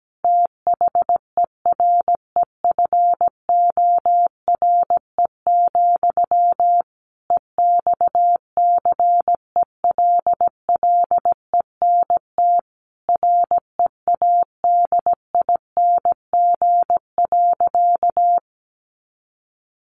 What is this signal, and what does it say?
Morse code, heard as a keyed tone, about 17 wpm, 700 Hz. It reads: THEREFORE, EXCELLENT READING.
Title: Sentences from Top 1000 Words 17wpm